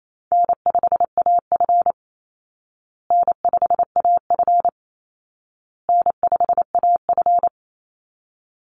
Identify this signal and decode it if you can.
Morse code, heard as a keyed tone, about 28 wpm, 700 Hz. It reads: N5AF N5AF N5AF
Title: Easy Call Signs Repeated 3x - 28wpm